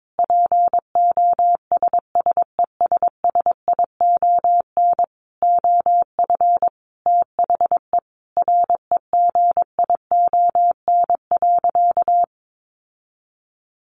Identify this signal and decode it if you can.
Morse code, heard as a keyed tone, about 22 wpm, 700 Hz. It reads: POSSESSION OF THE REGION.
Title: Sentences from Top 1500 Words 22wpm